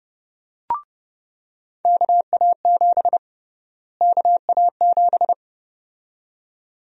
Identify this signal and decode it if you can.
Morse code, heard as a keyed tone, about 30 wpm, 700 Hz. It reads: KA7 KA7
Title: USA Call Sign Prefixes - Repeated 2x - 30wpm